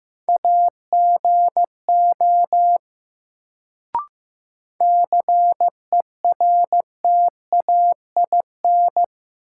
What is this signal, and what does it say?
Morse code, heard as a keyed tone, about 15 wpm, 700 Hz. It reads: AGO CERTAIN
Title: Sentences from Top 500 Words - New Word Drill - 15wpm